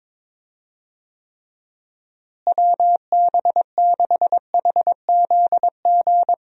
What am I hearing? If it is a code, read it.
Morse code, heard as a keyed tone, about 22 wpm, 700 Hz. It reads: WB6HZG
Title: USA Call Signs - 6 Characters - Repeated 3x - 22wpm